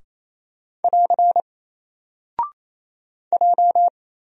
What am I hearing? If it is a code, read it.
Morse code, heard as a keyed tone, about 27 wpm, 700 Hz. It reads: <AR> J
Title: Single Character (Letter:Number:Punctuation:Prosign) 28wpm